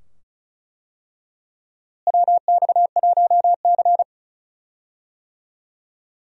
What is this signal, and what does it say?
Morse code, heard as a keyed tone, about 34 wpm, 700 Hz. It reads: WX1C
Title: USA Call Signs - 4 Characters - Repeated 2x - 35wpm